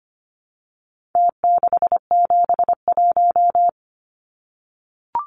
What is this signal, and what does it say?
Morse code, heard as a keyed tone, about 25 wpm, 700 Hz. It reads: T671
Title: Numbers - Contest Style - 4-Digit 25wpm